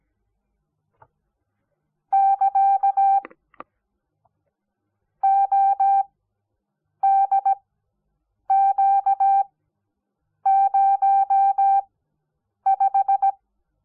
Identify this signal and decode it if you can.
Morse code, heard as a keyed tone, about 17 wpm, spaced as for 7 wpm, 780 Hz. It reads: <KA> ODQ05
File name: Onlinelehrgang / morsen / DJ4UF Morsen05